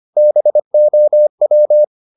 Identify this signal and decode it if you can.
Morse code, heard as a keyed tone, about 25 wpm, 600 Hz. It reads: BOW
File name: the old man and the sea part 1 25wpm